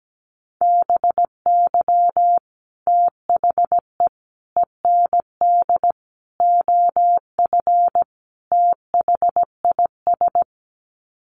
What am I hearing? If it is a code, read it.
Morse code, heard as a keyed tone, about 17 wpm, 700 Hz. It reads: BY THE END OF THIS
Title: Sets of 5 Words Encore 17wpm